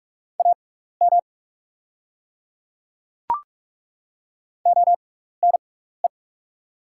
Morse code, heard as a keyed tone, about 45 wpm, spaced as for 15 wpm, 690 Hz. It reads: AM ONE
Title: 6x Character Spacing - Top 500 Words - 45wpm